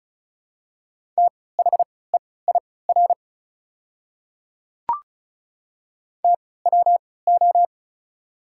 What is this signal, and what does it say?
Morse code, heard as a keyed tone, about 35 wpm, spaced as for 20 wpm, 700 Hz. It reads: THEIR TWO
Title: 3x Character Spacing - Top 100 Words - 35wpm